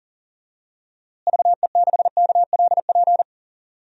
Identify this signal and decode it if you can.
Morse code, heard as a keyed tone, about 40 wpm, 700 Hz. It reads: VE6KLP
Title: DXCC-100 Call Signs - 6 Characters - Repeated 2x - 40wpm